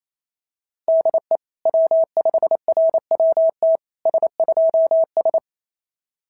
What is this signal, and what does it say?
Morse code, heard as a keyed tone, about 28 wpm, 690 Hz. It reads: DE W5RWT S2S
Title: SOTA QSO Practice 28wpm